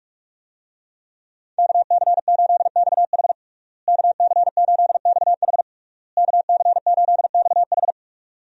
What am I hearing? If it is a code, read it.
Morse code, heard as a keyed tone, about 46 wpm, 700 Hz. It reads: KC8XH KC8XH KC8XH
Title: USA Call Signs - 5 Characters - Repeated 3x - 45wpm